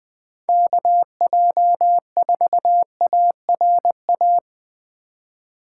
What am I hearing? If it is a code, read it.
Morse code, heard as a keyed tone, about 19 wpm, 700 Hz. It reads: KJ4ARA